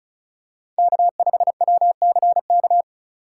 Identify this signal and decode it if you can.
Morse code, heard as a keyed tone, about 36 wpm, 700 Hz. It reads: K5WCK
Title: USA Call Signs - 5 Characters - 4x Character Spacing - 35wpm